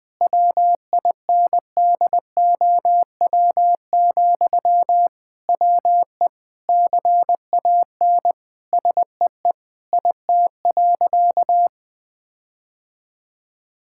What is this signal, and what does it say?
Morse code, heard as a keyed tone, about 20 wpm, 700 Hz. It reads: WINDOW, WE CAN SEE IT.
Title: Sentences from Top 1600 Words 20wpm